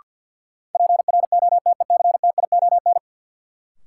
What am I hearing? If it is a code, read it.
Morse code, heard as a keyed tone, about 52 wpm, 700 Hz. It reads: PROTECTION
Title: Sentences from Top 1600 Words - New Word Drill Alt - 50wpm